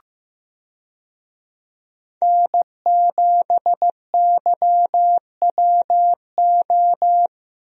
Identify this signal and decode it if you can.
Morse code, heard as a keyed tone, about 15 wpm, 700 Hz. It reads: N7YWO
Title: USA Call Signs - 5 Characters - Repeated 3x - 15wpm